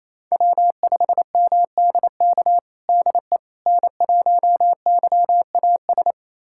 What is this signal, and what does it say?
Morse code, heard as a keyed tone, about 28 wpm, 700 Hz. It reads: W5MDK DE N1YAS
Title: USA Callsign DE Callsign - 5 Characters - 28wpm